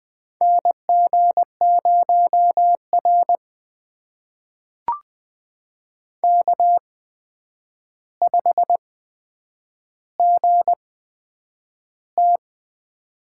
Morse code, heard as a keyed tone, about 20 wpm, spaced as for 9 wpm, 700 Hz. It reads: NG0R K5GT
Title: USA Call Signs - 4 Characters - 8x Character Spacing - 20wpm